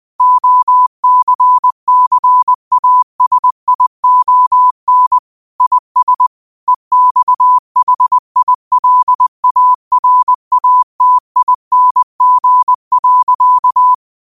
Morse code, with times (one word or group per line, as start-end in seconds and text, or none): 0.2-5.2 OCCASION
5.6-6.3 IS
6.7-13.9 EXHILARATING.